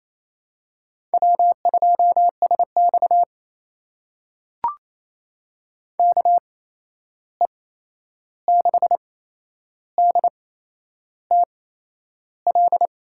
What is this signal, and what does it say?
Morse code, heard as a keyed tone, about 28 wpm, spaced as for 11 wpm, 700 Hz. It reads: W2SX KE6DTL
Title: USA Call Signs - 8x Character Spacing - 28wpm